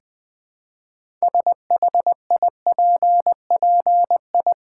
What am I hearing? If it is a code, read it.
Morse code, heard as a keyed tone, about 20 wpm, 700 Hz. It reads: SHIPPI
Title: Build A Word - 2 Letters At A Time - 8-Letter Words - 20wpm